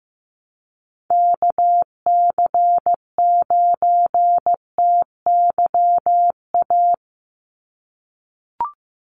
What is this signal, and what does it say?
Morse code, heard as a keyed tone, about 15 wpm, 700 Hz. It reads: KC9TYA
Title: NA SOTA Chaser Callsigns 15wpm